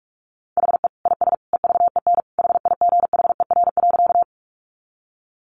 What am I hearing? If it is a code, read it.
Morse code, heard as a keyed tone, about 44 wpm, 700 Hz. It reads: HE IS EVEN HIGHER.
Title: Additional Easy Phrases 45wpm